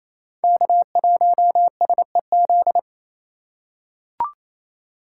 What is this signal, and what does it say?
Morse code, heard as a keyed tone, about 28 wpm, 700 Hz. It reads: K1SEZ
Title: USA Call Signs - 5 Characters - 4x Character Spacing - 28wpm